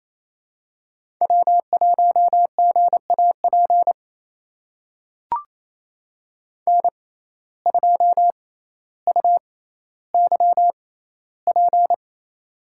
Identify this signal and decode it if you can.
Morse code, heard as a keyed tone, about 28 wpm, spaced as for 14 wpm, 700 Hz. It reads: W1GAP N2UYP
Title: USA Call Signs - 5 Characters - 6x Character Spacing - 28wpm